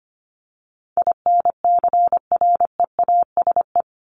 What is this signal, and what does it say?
Morse code, heard as a keyed tone, about 25 wpm, 700 Hz. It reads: INCREASE
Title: LY Suffix - Top 100 Words with Base Words 25wpm